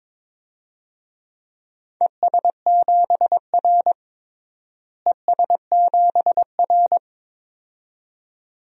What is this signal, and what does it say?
Morse code, heard as a keyed tone, about 22 wpm, 700 Hz. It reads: ES7R ES7R
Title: DXCC-100 Call Signs - 4 Characters - Repeated 2x - 22wpm